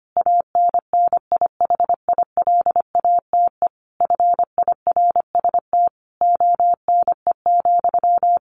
Morse code, heard as a keyed tone, about 25 wpm, 700 Hz. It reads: ANNIHILATE FIRST ONE,
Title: A Sound Of Thunder 25wpm